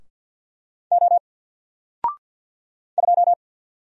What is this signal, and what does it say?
Morse code, heard as a keyed tone, about 49 wpm, 700 Hz. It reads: O 2
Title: Single Character (Letter:Number:Punctuation:Prosign) 50wpm